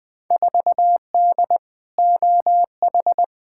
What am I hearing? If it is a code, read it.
Morse code, heard as a keyed tone, about 20 wpm, 700 Hz. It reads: 4D OH
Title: Field Day Exchanges Encore - 20wpm